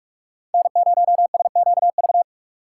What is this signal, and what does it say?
Morse code, heard as a keyed tone, about 44 wpm, 700 Hz. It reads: N0SQV
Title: USA Call Signs - 2x Character Spacing - 45wpm